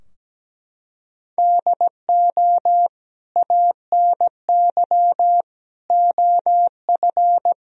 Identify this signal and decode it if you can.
Morse code, heard as a keyed tone, about 17 wpm, 700 Hz. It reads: DO ANY OF
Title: Sentences from Top 1000 Words - Review Part 3 - 17wpm